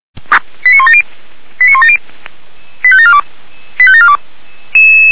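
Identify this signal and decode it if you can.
Five-tone selective call (ZVEI), heard as quick runs of five tones, tones of about 70 ms. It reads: 89170 89170 85742 85742
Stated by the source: ZVEI five-tone selective call 89185, received by SDR.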